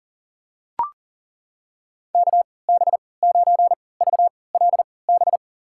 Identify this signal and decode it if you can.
Morse code, heard as a keyed tone, about 40 wpm, spaced as for 23 wpm, 700 Hz. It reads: KB9VLB
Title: USA Call Signs - 6 Characters - 3x Character Spacing - 40wpm